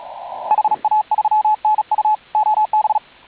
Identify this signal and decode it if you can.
Morse code, heard as a keyed tone, about 36 wpm, 840 Hz. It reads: HA3NU OB